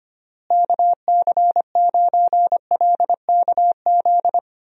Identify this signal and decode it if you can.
Morse code, heard as a keyed tone, about 25 wpm, 700 Hz. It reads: KC9LKZ
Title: USA Call Signs - 6 Characters - Repeated 2x - 25wpm